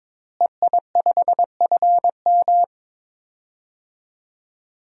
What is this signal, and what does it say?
Morse code, heard as a keyed tone, about 22 wpm, 700 Hz. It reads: EI5FM
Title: DXCC-100 Call Signs - 5 Characters - Repeated 2x - 22wpm